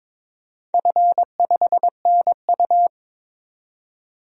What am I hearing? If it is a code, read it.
Morse code, heard as a keyed tone, about 22 wpm, 700 Hz. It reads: F5NU
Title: DXCC-100 Call Signs - 4 Characters - 22wpm